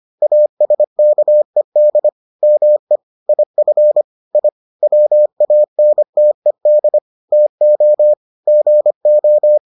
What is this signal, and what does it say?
Morse code, heard as a keyed tone, about 25 wpm, 600 Hz. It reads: ASKED ME IF I WANTED TO GO